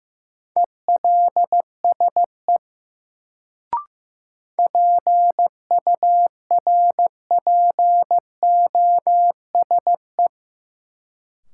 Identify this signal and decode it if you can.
Morse code, heard as a keyed tone, about 15 wpm, 700 Hz. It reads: ELSE PURPOSE
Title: Sentences from Top 800 Words - New Word Drill - 15wpm